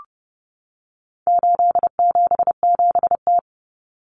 Morse code, heard as a keyed tone, about 30 wpm, 700 Hz. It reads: 877T
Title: Common Cut Numbers - 4-Digit 30wpm